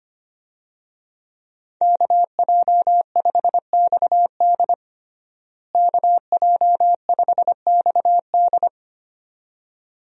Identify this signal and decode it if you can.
Morse code, heard as a keyed tone, about 25 wpm, 700 Hz. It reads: KJ5XD KJ5XD
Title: USA Call Signs - 5 Characters - Repeated 2x - 25wpm